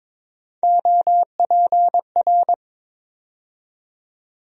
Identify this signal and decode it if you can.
Morse code, heard as a keyed tone, about 22 wpm, 700 Hz. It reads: OPR
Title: USA Call Sign Postfixes - 3 Letters 22wpm